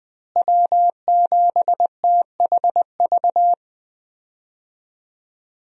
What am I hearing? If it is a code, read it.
Morse code, heard as a keyed tone, about 20 wpm, 700 Hz. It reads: W7THV
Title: USA Call Signs - 5 Characters - Repeated 2x - 20wpm